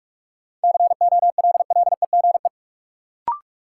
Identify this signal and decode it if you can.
Morse code, heard as a keyed tone, about 46 wpm, 700 Hz. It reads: COLLEGE
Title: Sentences from Top 1300 Words - New Word Drill - 45wpm